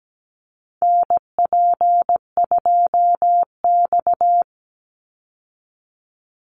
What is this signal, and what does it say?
Morse code, heard as a keyed tone, about 17 wpm, 700 Hz. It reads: NP2X
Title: Call Signs Repeated 2x - 17wpm